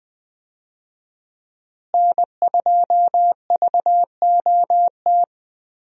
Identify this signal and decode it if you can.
Morse code, heard as a keyed tone, about 20 wpm, 700 Hz. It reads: N2VOT
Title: USA Call Signs - 5 Characters - Repeated 3x - 20wpm